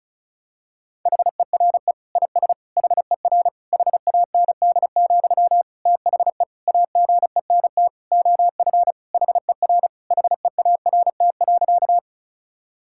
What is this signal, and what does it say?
Morse code, heard as a keyed tone, about 34 wpm, 700 Hz. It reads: HERE IS HER HAND, THE AGENT OF HER HEART.
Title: Sentences from Top 1600 Words 35wpm